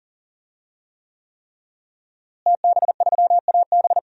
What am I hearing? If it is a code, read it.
Morse code, heard as a keyed tone, about 40 wpm, 700 Hz. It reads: TB3AB
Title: DXCC-100 Call Signs - 5 Characters - Repeated 3x - 40wpm